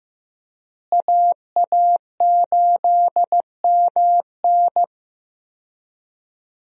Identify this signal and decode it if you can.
Morse code, heard as a keyed tone, about 15 wpm, 700 Hz. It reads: AA8MN
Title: USA Call Signs - 5 Characters - Repeated 3x - 15wpm